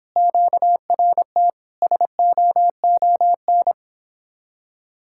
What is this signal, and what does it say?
Morse code, heard as a keyed tone, about 26 wpm, 700 Hz. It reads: QRT SOON